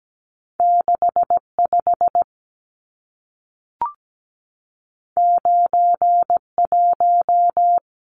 Morse code, heard as a keyed tone, about 17 wpm, 700 Hz. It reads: 65 91
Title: Numbers - 2-Digit 17wpm